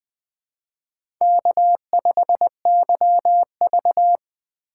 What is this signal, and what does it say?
Morse code, heard as a keyed tone, about 21 wpm, 700 Hz. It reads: K5YV